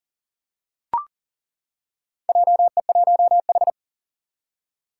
Morse code, heard as a keyed tone, about 40 wpm, 700 Hz. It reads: JE1H
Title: Four Challenging Letter-Number Combinations 40wpm